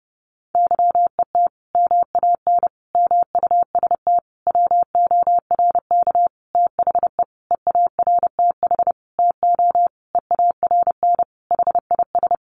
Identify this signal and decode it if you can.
Morse code, heard as a keyed tone, about 30 wpm, 700 Hz. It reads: YET MAN MUST WORK THE EARTH TO EARN HIS
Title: Sentences from Top 1500 Words 30wpm